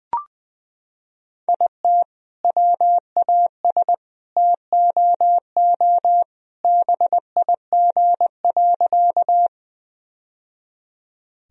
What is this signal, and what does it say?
Morse code, heard as a keyed tone, about 21 wpm, 700 Hz. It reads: IT WAS TOO BIG.
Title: Sentences from Top 1000 Words - Review Part 1 - 20wpm